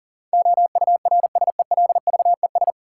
Morse code, heard as a keyed tone, about 40 wpm, 700 Hz. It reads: OURSELVES
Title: Sentences from Top 1600 Words - New Word Drill Alt - 40wpm